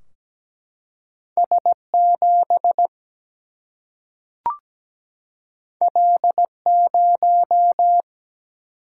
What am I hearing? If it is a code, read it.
Morse code, heard as a keyed tone, about 17 wpm, 700 Hz. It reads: S7 L0
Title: Single-Letter Single-Number 17wpm